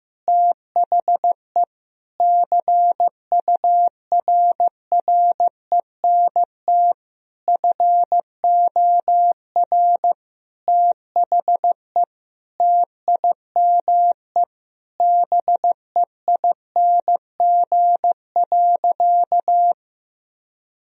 Morse code, heard as a keyed tone, about 15 wpm, 700 Hz. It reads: THE CURRENT FOR THE TIME BEING.
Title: Sentences from Top 1000 Words - Review Part 4 - 15wpm